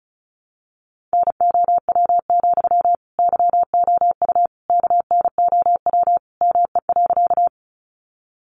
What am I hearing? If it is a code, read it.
Morse code, heard as a keyed tone, about 34 wpm, 700 Hz. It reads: NOW, YOU KNOW ME.
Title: Sentences from Top 200 Words 35wpm